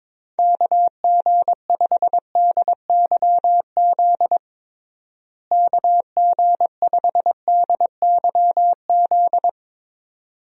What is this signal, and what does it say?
Morse code, heard as a keyed tone, about 22 wpm, 700 Hz. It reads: KG5DYZ KG5DYZ
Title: USA Call Signs - 6 Characters - Repeated 2x - 22wpm